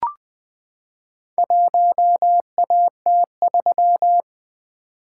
Morse code, heard as a keyed tone, about 21 wpm, 700 Hz. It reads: 1AT3
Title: Four Letter-Number Combinations - 20wpm